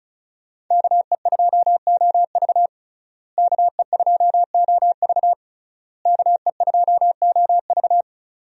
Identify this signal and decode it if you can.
Morse code, heard as a keyed tone, about 36 wpm, 700 Hz. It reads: KE2OV KE2OV KE2OV
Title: USA Call Signs - 5 Characters - Repeated 3x - 35wpm